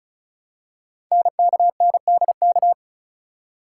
Morse code, heard as a keyed tone, about 36 wpm, 700 Hz. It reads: NKNDK
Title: Code Groups - 5 Letters - 35wpm